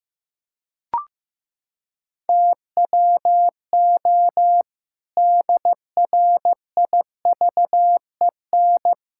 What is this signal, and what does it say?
Morse code, heard as a keyed tone, about 15 wpm, 700 Hz. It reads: TWO DRIVEN
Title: Pangram Challenge v2 - 15wpm